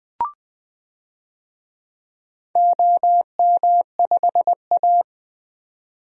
Morse code, heard as a keyed tone, about 20 wpm, 700 Hz. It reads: OM5A
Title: DXCC-100 Call Signs - 4 Characters - 20wpm